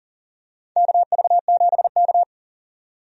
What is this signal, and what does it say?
Morse code, heard as a keyed tone, about 40 wpm, 700 Hz. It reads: KV7K
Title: USA Call Signs - 4 Characters - 40wpm